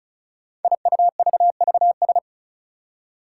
Morse code, heard as a keyed tone, about 36 wpm, 700 Hz. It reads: IUVVS
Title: Five Challenging Letter-Number Combinations 35wpm